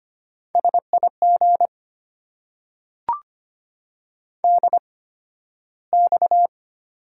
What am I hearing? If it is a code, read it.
Morse code, heard as a keyed tone, about 25 wpm, spaced as for 13 wpm, 700 Hz. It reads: SIG DX
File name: Top 90 QSO Elements - 8x Character Spacing - 25wpm